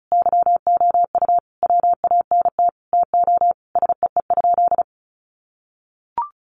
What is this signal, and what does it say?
Morse code, heard as a keyed tone, about 34 wpm, 700 Hz. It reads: YOU WANT TO SEE?
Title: Sentences from Top 1100 Words 35wpm